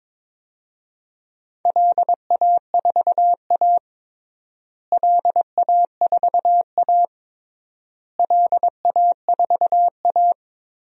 Morse code, heard as a keyed tone, about 22 wpm, 700 Hz. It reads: LA4A LA4A LA4A
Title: DXCC-100 Call Signs - 4 Characters - Repeated 3x - 22wpm